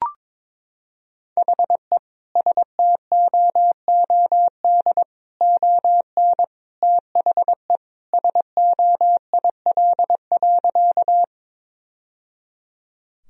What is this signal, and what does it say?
Morse code, heard as a keyed tone, about 22 wpm, 700 Hz. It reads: HE STOOD ON THE SOIL.
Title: Sentences from Top 1000 Words - Review Part 4 - 22wpm